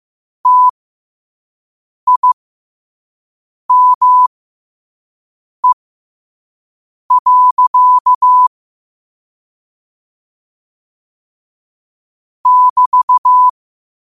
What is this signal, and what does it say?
Morse code, heard as a keyed tone, about 15 wpm, spaced as for 5 wpm, 1000 Hz. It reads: TIME. <BT>
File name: qotdfiles / 5wpm